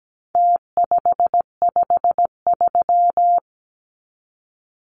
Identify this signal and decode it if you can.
Morse code, heard as a keyed tone, about 17 wpm, 700 Hz. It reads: T553
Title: Numbers - Contest Style - 4-Digit 17wpm